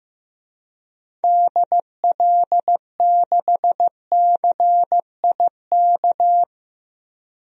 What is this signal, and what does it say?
Morse code, heard as a keyed tone, about 14 wpm, 700 Hz. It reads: DL6CIK
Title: DXCC-100 Call Signs - 6 Characters - Repeated 3x - 15wpm